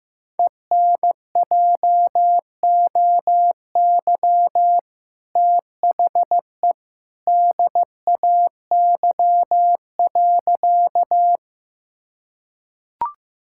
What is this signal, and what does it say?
Morse code, heard as a keyed tone, about 15 wpm, 700 Hz. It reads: ENJOY THE DAY.
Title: Sentences from Top 1000 Words - Review Part 5 - 15wpm